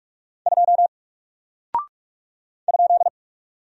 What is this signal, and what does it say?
Morse code, heard as a keyed tone, about 46 wpm, 700 Hz. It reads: 2 ?
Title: Single Character (Letter:Number:Punctuation:Prosign) Alt 45wpm